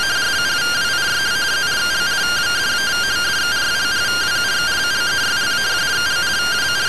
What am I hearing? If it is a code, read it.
Two-tone teleprinter signal, about 170 Hz shift.